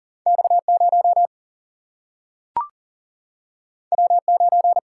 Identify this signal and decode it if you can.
Morse code, heard as a keyed tone, about 39 wpm, 700 Hz. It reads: X0 W9
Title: Single-Letter Single-Number 40wpm